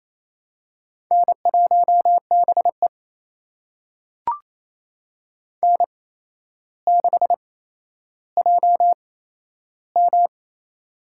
Morse code, heard as a keyed tone, about 28 wpm, spaced as for 12 wpm, 700 Hz. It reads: N1BE N6JM
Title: USA Call Signs - 4 Characters - 8x Character Spacing - 28wpm